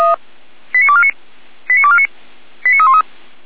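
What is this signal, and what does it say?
Five-tone selective call (ZVEI), heard as quick runs of five tones, tones of about 70 ms. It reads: 89249 89249 89313
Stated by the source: ZVEI five-tone selective call received on SDR, address 89310.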